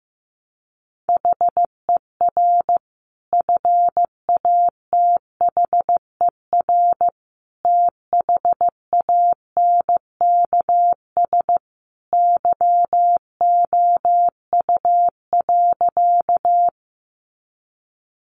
Morse code, heard as a keyed tone, about 15 wpm, 700 Hz. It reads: HER FATHER THANKS YOU.
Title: Sentences from Top 600 Words 15wpm